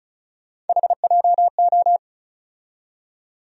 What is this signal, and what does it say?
Morse code, heard as a keyed tone, about 36 wpm, 700 Hz. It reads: HJO
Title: USA Call Sign Postfixes - 3 Letters 35wpm